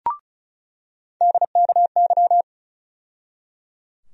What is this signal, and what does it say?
Morse code, heard as a keyed tone, about 34 wpm, 700 Hz. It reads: DKY
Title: USA Call Sign Postfixes - 3 Letters 35wpm